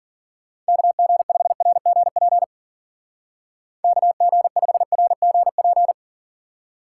Morse code, heard as a keyed tone, about 52 wpm, 700 Hz. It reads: KG5RGP KG5RGP